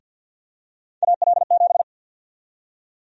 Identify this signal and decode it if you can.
Morse code, heard as a keyed tone, about 49 wpm, 700 Hz. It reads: AL7
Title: USA Call Sign Prefixes - Repeated 2x - 50wpm